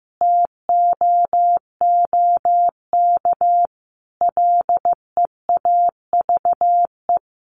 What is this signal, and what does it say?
Morse code, heard as a keyed tone, about 15 wpm, 700 Hz. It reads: TOOK LEAVE
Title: Sentences from Top 1400 Words 15wpm